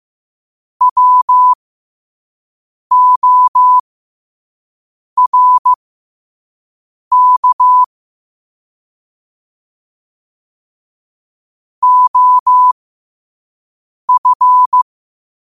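Morse code, heard as a keyed tone, about 15 wpm, spaced as for 5 wpm, 1000 Hz. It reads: WORK OF
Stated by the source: Morse code at 5 wpm.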